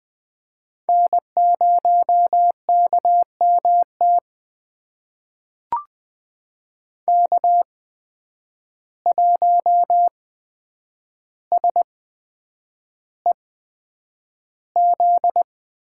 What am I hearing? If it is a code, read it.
Morse code, heard as a keyed tone, about 20 wpm, spaced as for 9 wpm, 700 Hz. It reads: N0KMT K1SEZ
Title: USA Call Signs - 5 Characters - 8x Character Spacing - 20wpm